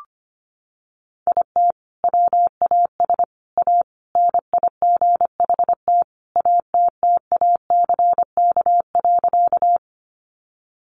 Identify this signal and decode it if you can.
Morse code, heard as a keyed tone, about 25 wpm, 700 Hz. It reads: IT WAS A NIGHT ATTACK.
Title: Sentences from Top 1500 Words 25wpm